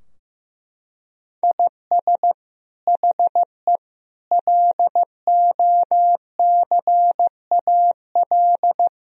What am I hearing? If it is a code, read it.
Morse code, heard as a keyed tone, about 15 wpm, 700 Hz. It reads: IS HE LOCAL
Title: Sentences from Top 1000 Words - Review Part 3 - 15wpm